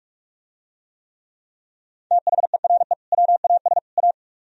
Morse code, heard as a keyed tone, about 47 wpm, 700 Hz. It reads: THERE WAS A